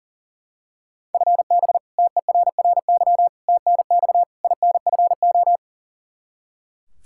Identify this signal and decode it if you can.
Morse code, heard as a keyed tone, about 40 wpm, 700 Hz. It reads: FB TERRY TNX INFO
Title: Practice QSO - Exchange 3 - More Info - 40wpm